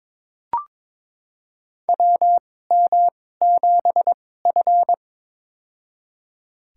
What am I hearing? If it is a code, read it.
Morse code, heard as a keyed tone, about 22 wpm, spaced as for 16 wpm, 700 Hz. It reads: WM7F
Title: USA Call Signs - 2x Character Spacing - 22wpm